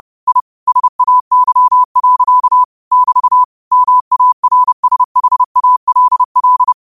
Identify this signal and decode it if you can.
Morse code, heard as a keyed tone, about 30 wpm, 1000 Hz. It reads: I SAY. <BT> MARSHALL
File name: qotdfiles / 30wpm